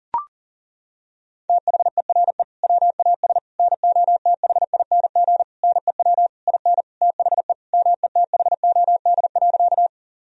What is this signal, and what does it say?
Morse code, heard as a keyed tone, about 37 wpm, 700 Hz. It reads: THERE WAS NOTHING NEW IN THE METHOD.
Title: Sentences from Top 1000 Words - Review Part 3 - 40wpm